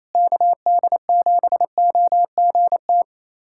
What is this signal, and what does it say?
Morse code, heard as a keyed tone, about 28 wpm, 700 Hz. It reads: KD7OGT
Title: USA Callsign DE Callsign - 6 Characters - 28wpm